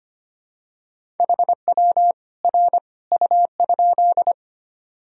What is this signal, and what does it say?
Morse code, heard as a keyed tone, about 25 wpm, 700 Hz. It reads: HW R U?
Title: Sentences from Top 100 Words and Abbreviations 25wpm